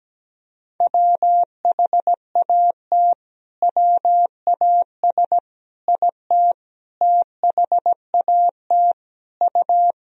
Morse code, heard as a keyed tone, about 17 wpm, 700 Hz. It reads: WHAT WAS IT THAT U
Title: Sentences from Top 300 Words and Abbreviations 17wpm